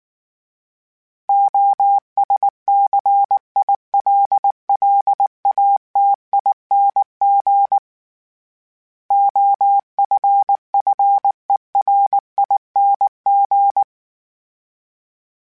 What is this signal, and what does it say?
Morse code, heard as a keyed tone, about 19 wpm, 800 Hz. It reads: OSCILLATING OFFERING
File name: Special INGs / Special INGs-mp3s / Sngl Ing @18